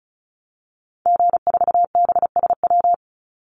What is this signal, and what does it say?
Morse code, heard as a keyed tone, about 34 wpm, 700 Hz. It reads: G4BSW
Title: Call Signs Repeated 2x - 35wpm